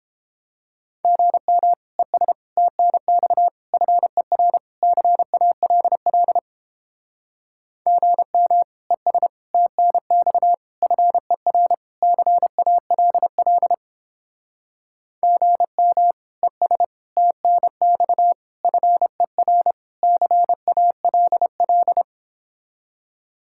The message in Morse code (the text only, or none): GM ES TNX FER CALL GM ES TNX FER CALL GM ES TNX FER CALL